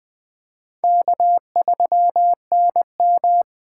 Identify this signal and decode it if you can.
Morse code, heard as a keyed tone, about 25 wpm, 690 Hz. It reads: K3NM
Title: Speed Racing - USA 1x2 Call Signs 20wpm